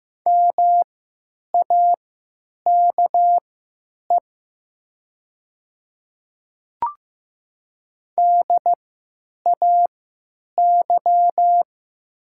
Morse code, heard as a keyed tone, about 15 wpm, spaced as for 9 wpm, 700 Hz. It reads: MAKE DAY
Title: 3x Character Spacing - Top 100 Words - 15wpm